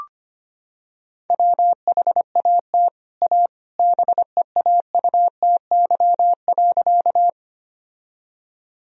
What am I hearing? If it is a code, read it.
Morse code, heard as a keyed tone, about 25 wpm, 700 Hz. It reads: WHAT A BEAUTY.
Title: Sentences from Top 1000 Words - Review Part 2 - 25wpm